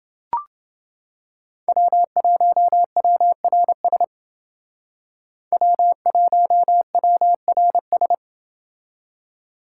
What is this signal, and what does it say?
Morse code, heard as a keyed tone, about 29 wpm, 700 Hz. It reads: W1WRS W1WRS